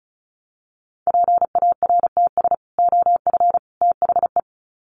Morse code, heard as a keyed tone, about 34 wpm, 700 Hz. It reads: PARTS OF THE
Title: Sets of 3 Words 35wpm